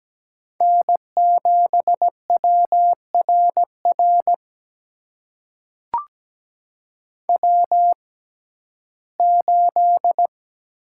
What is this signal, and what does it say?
Morse code, heard as a keyed tone, about 17 wpm, spaced as for 12 wpm, 700 Hz. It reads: N7WRR W8
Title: USA Call Signs - 5 Characters - 6x Character Spacing - 17wpm